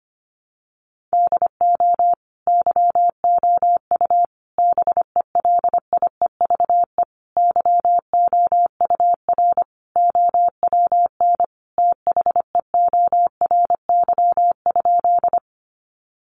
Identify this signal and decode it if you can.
Morse code, heard as a keyed tone, about 25 wpm, 700 Hz. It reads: DO YOU BELIEVE YOUR OWN THEORY?
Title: Sentences from Top 1000 Words 25wpm